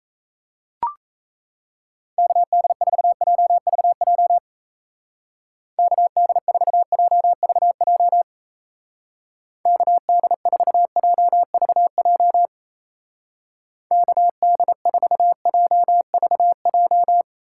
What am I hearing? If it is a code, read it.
Morse code, heard as a keyed tone, about 35 wpm, 700 Hz. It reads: KD4JVJ KD4JVJ KD4JVJ KD4JVJ
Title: Speed Racing - USA 2x3 Call Signs 28wpm